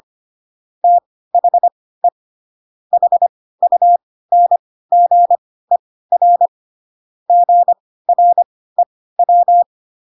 Morse code, heard as a keyed tone, about 25 wpm, spaced as for 16 wpm, 700 Hz. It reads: THE HUNGER GREW